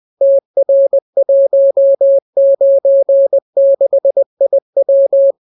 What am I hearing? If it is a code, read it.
Morse code, heard as a keyed tone, about 20 wpm, 550 Hz. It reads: TR196IW